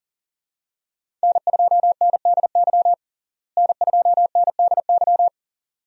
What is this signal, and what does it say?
Morse code, heard as a keyed tone, about 40 wpm, 700 Hz. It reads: N2NDY N2NDY
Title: USA Call Signs - 5 Characters - Repeated 2x - 40wpm